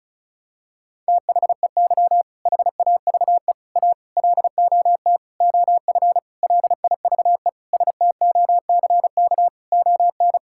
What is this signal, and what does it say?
Morse code, heard as a keyed tone, about 36 wpm, 700 Hz. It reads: THEY HAVE A LOT OF LIVE STOCK ON
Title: Sentences from Top 1000 Words - Review Part 5 - 35wpm